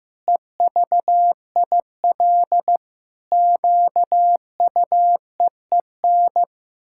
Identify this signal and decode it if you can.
Morse code, heard as a keyed tone, about 15 wpm, 700 Hz. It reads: EVIL QUEEN
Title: Pangram Challenge v2 - 15wpm